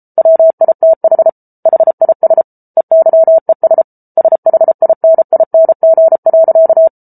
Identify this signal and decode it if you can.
Morse code, heard as a keyed tone, about 34 wpm, 650 Hz. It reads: WITH HIS EYES SHINING.